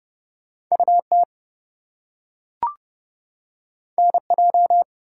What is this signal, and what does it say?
Morse code, heard as a keyed tone, about 30 wpm, 700 Hz. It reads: UT NJ
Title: US States and Canadian Provinces 30wpm